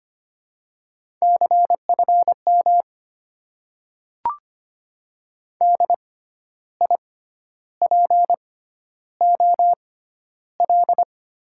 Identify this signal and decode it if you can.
Morse code, heard as a keyed tone, about 25 wpm, spaced as for 11 wpm, 690 Hz. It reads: CFM DIPOL
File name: Top 90 QSO Elements - 6x Character Spacing - 25wpm